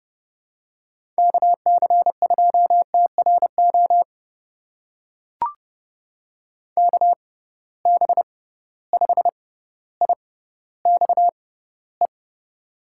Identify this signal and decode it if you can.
Morse code, heard as a keyed tone, about 30 wpm, spaced as for 15 wpm, 700 Hz. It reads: KC2TRO KB5IXE
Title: USA Call Signs - 6 Characters - 6x Character Spacing - 30wpm